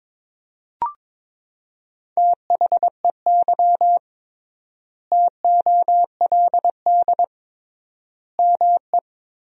Morse code, heard as a keyed tone, about 22 wpm, 700 Hz. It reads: THEY TOLD ME
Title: 3x Word-Spacing - Sentences from Top 400 Words - 22wpm